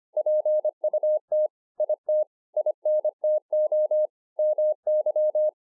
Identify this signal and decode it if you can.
Morse code, heard as a keyed tone, about 25 wpm, 610 Hz. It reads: PUT IT INTO MY